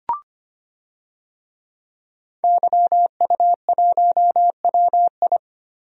Morse code, heard as a keyed tone, about 25 wpm, 700 Hz. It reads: YU1WI
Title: DXCC-100 Call Signs - 5 Characters - Repeated 2x - 25wpm